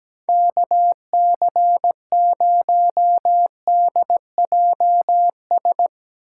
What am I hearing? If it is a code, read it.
Morse code, heard as a keyed tone, about 17 wpm, 700 Hz. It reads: KC0DJS
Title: USA Call Signs - 8x Character Spacing - 17wpm